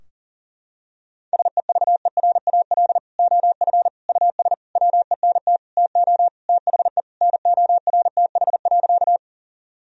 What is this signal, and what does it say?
Morse code, heard as a keyed tone, about 40 wpm, 700 Hz. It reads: SEVERAL OF US WENT TO THE NORTH.
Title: Sentences from Top 1000 Words - Review Part 2 - 40wpm